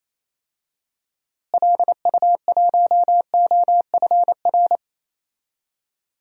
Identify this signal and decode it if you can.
Morse code, heard as a keyed tone, about 27 wpm, 700 Hz. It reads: LU1OFR
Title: DXCC-100 Call Signs - 6 Characters - Repeated 3x - 28wpm